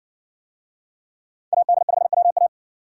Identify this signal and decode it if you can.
Morse code, heard as a keyed tone, about 56 wpm, 700 Hz. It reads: AD5WA